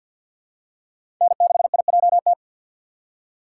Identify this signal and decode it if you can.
Morse code, heard as a keyed tone, about 48 wpm, 700 Hz. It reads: N6IJT
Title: USA Call Signs - 5 Characters - 6x Character Spacing - 50wpm